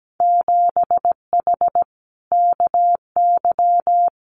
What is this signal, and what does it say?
Morse code, heard as a keyed tone, about 17 wpm, 700 Hz. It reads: 7H KY
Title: Winter Field Day Exchanges Encore 17wpm